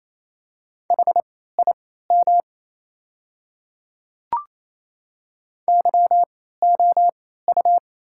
Morse code, heard as a keyed tone, about 27 wpm, spaced as for 16 wpm, 700 Hz. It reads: HIM YOU
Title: 3x Character Spacing - Top 200 Words - 28wpm